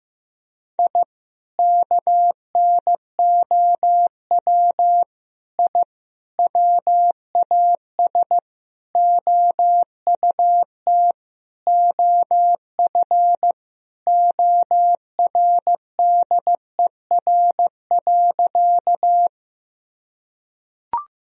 Morse code, heard as a keyed tone, about 15 wpm, 700 Hz. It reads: I KNOW I WAS OUT OF ORDER.
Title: Sentences from Top 1000 Words - Review Part 2 - 15wpm